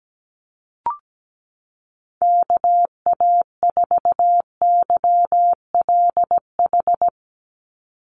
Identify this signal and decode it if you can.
Morse code, heard as a keyed tone, about 17 wpm, 700 Hz. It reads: KA4YLH
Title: USA Call Signs 17wpm